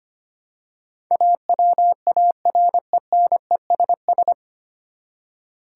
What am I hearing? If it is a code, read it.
Morse code, heard as a keyed tone, about 25 wpm, 700 Hz. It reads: AWARENESS
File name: Nine Letter Words - 25wpm